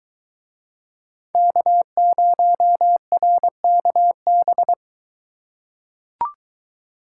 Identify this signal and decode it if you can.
Morse code, heard as a keyed tone, about 21 wpm, 700 Hz. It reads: K0RKB